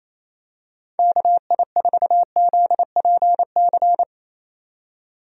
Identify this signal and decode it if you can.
Morse code, heard as a keyed tone, about 27 wpm, 700 Hz. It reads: KI4ZPC
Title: USA Call Signs - 6 Characters - 28wpm